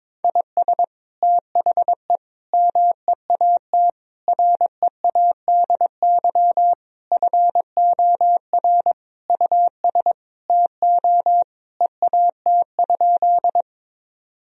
Morse code, heard as a keyed tone, about 21 wpm, 700 Hz. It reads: IS THE MEAT READY FOR US TO EAT?
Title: Sentences from Top 1000 Words - Review Part 3 - 22wpm